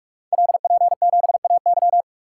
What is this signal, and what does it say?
Morse code, heard as a keyed tone, about 44 wpm, 700 Hz. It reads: LP7AY
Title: Code Groups - 5 Letter:Numbers - 45wpm